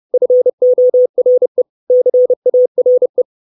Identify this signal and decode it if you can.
Morse code, heard as a keyed tone, about 30 wpm, 500 Hz. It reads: FORE CARE
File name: 632 Words 30WPM